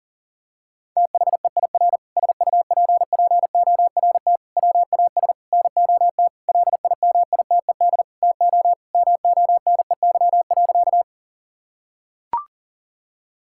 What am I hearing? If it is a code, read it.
Morse code, heard as a keyed tone, about 37 wpm, 700 Hz. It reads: THEIR SUPPORT WAS NOT LIMITED TO MONEY.
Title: Sentences from Top 1000 Words - Review Part 6 - 40wpm